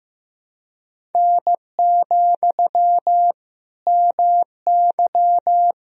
Morse code, heard as a keyed tone, about 15 wpm, 700 Hz. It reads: N, MY
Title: Sentences from Top 400 Words and Abbreviations 15wpm